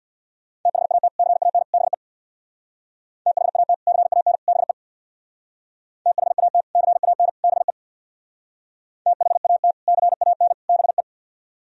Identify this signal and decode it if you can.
Morse code, heard as a keyed tone, about 58 wpm, 700 Hz. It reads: THAT CAN BE THAT CAN BE THAT CAN BE THAT CAN BE
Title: Speed-Racing - Sets of 3 Words 50wpm